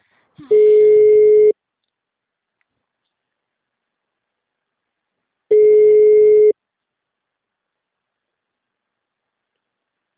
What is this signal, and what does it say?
Ringback tone (Europe, 425 Hz).